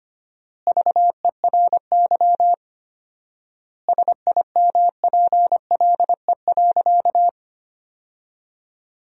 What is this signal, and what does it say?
Morse code, heard as a keyed tone, about 25 wpm, 700 Hz. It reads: VERY SIMPLE.
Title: 4x Word-Spacing - Sentences from Top 400 Words - 25wpm